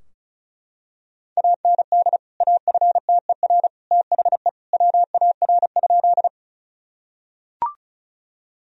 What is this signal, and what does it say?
Morse code, heard as a keyed tone, about 36 wpm, 700 Hz. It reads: AND AFTER THE WAR?
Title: Sentences from Top 1000 Words - Review Part 2 - 35wpm